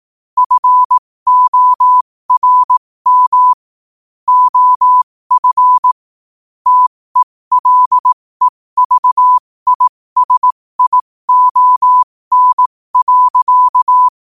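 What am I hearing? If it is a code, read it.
Morse code, heard as a keyed tone, about 18 wpm, spaced as for 15 wpm, 1000 Hz. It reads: FORM OF TELEVISION.
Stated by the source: Morse code quote of the day sent at 15 words per minute.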